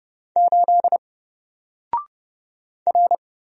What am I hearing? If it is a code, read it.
Morse code, heard as a keyed tone, about 31 wpm, 700 Hz. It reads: 8 R
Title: Single Letter-Number Alt 30wpm